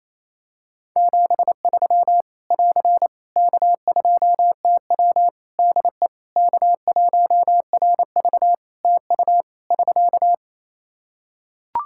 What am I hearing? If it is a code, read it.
Morse code, heard as a keyed tone, about 28 wpm, 700 Hz. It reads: 73 <AR> K2TW DE K1RV TU <SK>
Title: K1USN SST QSO Practice 28wpm